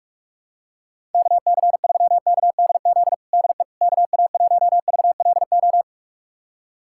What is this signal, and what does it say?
Morse code, heard as a keyed tone, about 46 wpm, 700 Hz. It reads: KC3KDZ DE KA1VLO
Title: USA Callsign DE Callsign - 6 Characters - 45wpm